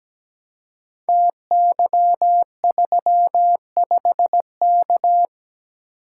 Morse code, heard as a keyed tone, about 17 wpm, 700 Hz. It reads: TY35K
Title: Code Groups - 5 Letter:Numbers - 17wpm